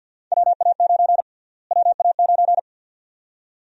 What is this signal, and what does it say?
Morse code, heard as a keyed tone, about 50 wpm, 700 Hz. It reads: WA9 WA9
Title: USA Call Sign Prefixes - Repeated 2x - 50wpm